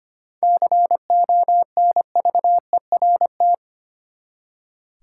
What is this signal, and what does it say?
Morse code, heard as a keyed tone, about 25 wpm, 700 Hz. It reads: CONVERT
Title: Seven Letter Words - 25wpm